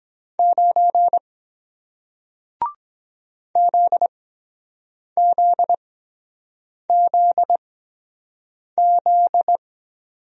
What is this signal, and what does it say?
Morse code, heard as a keyed tone, about 23 wpm, spaced as for 7 wpm, 700 Hz. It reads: 9 ZZZZ